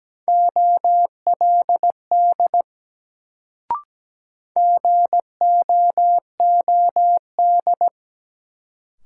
Morse code, heard as a keyed tone, about 17 wpm, 700 Hz. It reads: OLD GOOD
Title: Sentences from Top 200 Words - New Word Drill - 17wpm